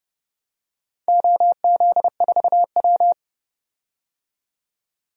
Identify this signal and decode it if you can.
Morse code, heard as a keyed tone, about 30 wpm, 700 Hz. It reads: OZ4W
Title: DXCC-100 Call Signs - 4 Characters - 30wpm